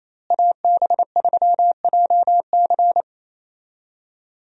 Morse code, heard as a keyed tone, about 28 wpm, 700 Hz. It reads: AB3JC
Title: USA Call Signs - 5 Characters - Repeated 2x - 28wpm